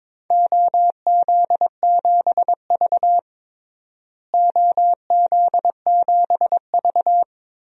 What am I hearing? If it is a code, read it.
Morse code, heard as a keyed tone, about 22 wpm, 700 Hz. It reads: OZ7V OZ7V
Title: DXCC-100 Call Signs - 4 Characters - Repeated 3x - 22wpm